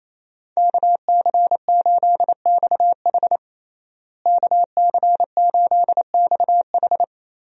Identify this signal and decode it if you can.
Morse code, heard as a keyed tone, about 28 wpm, 700 Hz. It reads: KC8XH KC8XH
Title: USA Call Signs - 5 Characters - Repeated 3x - 28wpm